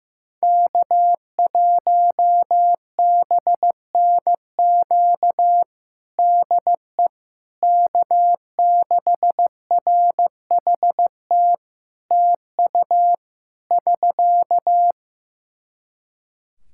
Morse code, heard as a keyed tone, about 15 wpm, 700 Hz. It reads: K1BNQ DE K6RHT TU <SK>
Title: K1USN SST QSO Practice 15wpm